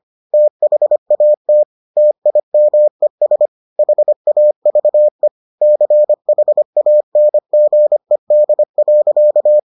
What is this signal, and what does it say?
Morse code, heard as a keyed tone, about 25 wpm, 600 Hz. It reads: THAT TIMES HAVE CHANGED.